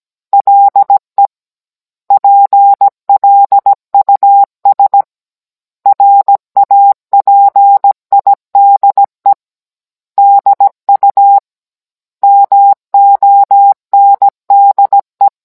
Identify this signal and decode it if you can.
Morse code, heard as a keyed tone, about 17 wpm, 800 Hz. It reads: LE PLUS RAPIDE DU MONDE